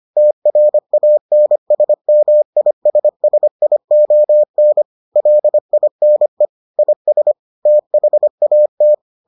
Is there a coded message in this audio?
Yes: Morse code from 0.2 to 8.9 s, heard as a keyed tone, about 25 wpm, 600 Hz. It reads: TRANSMISSION LINE IS THAT